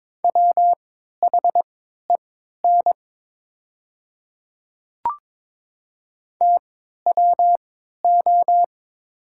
Morse code, heard as a keyed tone, about 23 wpm, spaced as for 13 wpm, 700 Hz. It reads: WHEN TWO
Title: 3x Character Spacing - Top 100 Words - 22wpm